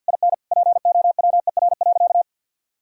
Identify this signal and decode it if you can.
Morse code, heard as a keyed tone, about 50 wpm, 700 Hz. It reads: IN POWER.
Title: Sentences from Top 1000 Words - Review Part 5 - 50wpm